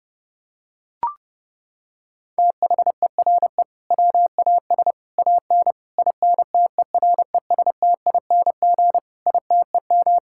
Morse code, heard as a keyed tone, about 30 wpm, 700 Hz. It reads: THERE WAS AN INTERESTING ITEM
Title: Sentences from Top 1000 Words - Review Part 6 - 30wpm